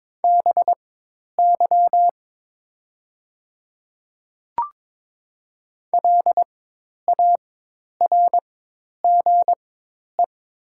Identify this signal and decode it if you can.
Morse code, heard as a keyed tone, about 22 wpm, spaced as for 10 wpm, 700 Hz. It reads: BY LARGE
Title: 4x Character Spacing - Top 200 Words - 22wpm